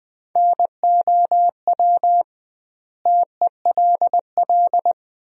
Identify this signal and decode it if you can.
Morse code, heard as a keyed tone, about 20 wpm, 700 Hz. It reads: NOW TELL
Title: 2x Word-Spacing - Sentences from Top 200 Words - 20wpm